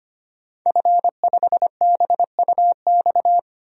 Morse code, heard as a keyed tone, about 25 wpm, 700 Hz. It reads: F5BUX
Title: DXCC-100 Call Signs - 5 Characters - 25wpm